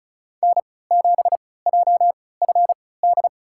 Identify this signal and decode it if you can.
Morse code, heard as a keyed tone, about 36 wpm, spaced as for 20 wpm, 700 Hz. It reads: N7JFD
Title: USA Call Signs - 3x Character Spacing - 35wpm